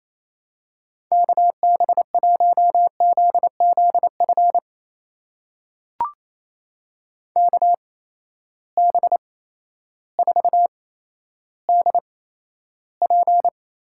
Morse code, heard as a keyed tone, about 28 wpm, spaced as for 13 wpm, 700 Hz. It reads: KB1ZZF KB4DP
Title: USA Call Signs - 6 Characters - 8x Character Spacing - 28wpm